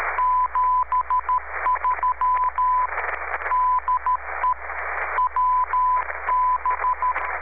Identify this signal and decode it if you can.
Morse code, heard as a keyed tone, about 13 wpm, 1000 Hz, with radio noise. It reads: 73 DE WB
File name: w6oqi-aug 29-s3